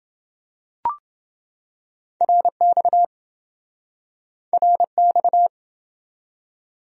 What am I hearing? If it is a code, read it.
Morse code, heard as a keyed tone, about 29 wpm, 700 Hz. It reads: RX RX